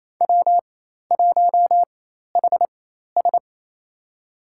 Morse code, heard as a keyed tone, about 28 wpm, spaced as for 13 wpm, 700 Hz. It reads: W1HS
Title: USA Call Signs - 4x Character Spacing - 28wpm